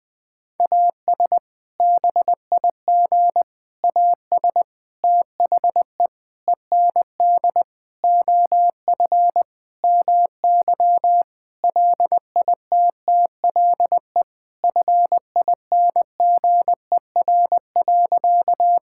Morse code, heard as a keyed tone, about 20 wpm, 700 Hz. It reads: AS BIG AS THE END OF MY LITTLE FINGER.
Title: Sentences from Top 1600 Words 20wpm